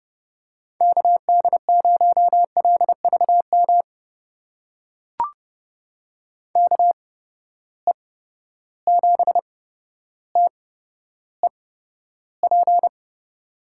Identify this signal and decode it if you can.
Morse code, heard as a keyed tone, about 31 wpm, spaced as for 13 wpm, 700 Hz. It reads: KD0LVM KE7TEP
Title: USA Call Signs - 6 Characters - 8x Character Spacing - 30wpm